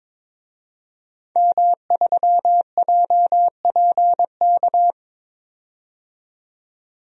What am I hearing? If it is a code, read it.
Morse code, heard as a keyed tone, about 22 wpm, 700 Hz. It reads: M3JPK
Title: DXCC-100 Call Signs - 5 Characters - 22wpm